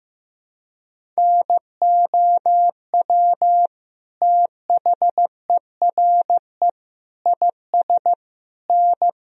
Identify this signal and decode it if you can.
Morse code, heard as a keyed tone, about 15 wpm, 700 Hz. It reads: NOW THERE IS N
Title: Sentences from Top 100 Words and Abbreviations 15wpm